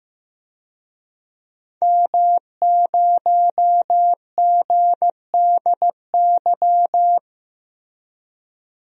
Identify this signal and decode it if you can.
Morse code, heard as a keyed tone, about 15 wpm, 700 Hz. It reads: M0GDY
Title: DXCC-100 Call Signs - 5 Characters - Repeated 3x - 15wpm